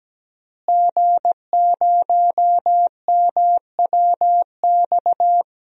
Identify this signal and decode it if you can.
Morse code, heard as a keyed tone, about 17 wpm, 700 Hz. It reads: G0MWX
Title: DXCC-100 Call Signs - 5 Characters - Repeated 2x - 17wpm